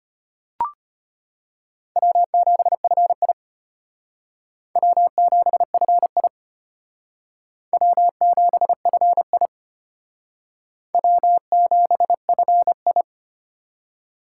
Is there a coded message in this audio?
Yes: Morse code from 2.0 to 13.0 s, heard as a keyed tone, about 32 wpm, 700 Hz. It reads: W7FI W7FI W7FI W7FI